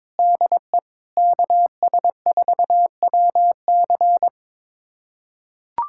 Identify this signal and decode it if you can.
Morse code, heard as a keyed tone, about 22 wpm, 700 Hz. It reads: DE KS4WC
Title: USA Callsign DE Callsign - 5 Characters - 22wpm